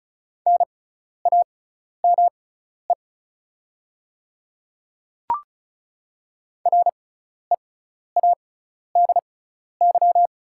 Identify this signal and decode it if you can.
Morse code, heard as a keyed tone, about 34 wpm, spaced as for 12 wpm, 690 Hz. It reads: NAME READY
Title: 6x Character Spacing - Top 500 Words - 35wpm